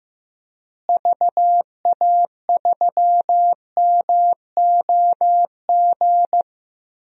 Morse code, heard as a keyed tone, about 15 wpm, 700 Hz. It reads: VA3MOG
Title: DXCC-100 Call Signs - 6 Characters - Repeated 2x - 15wpm